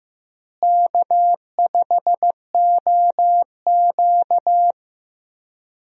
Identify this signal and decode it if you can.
Morse code, heard as a keyed tone, about 15 wpm, 700 Hz. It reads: K5OQ
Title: USA Call Signs - 4 Characters - Repeated 2x - 15wpm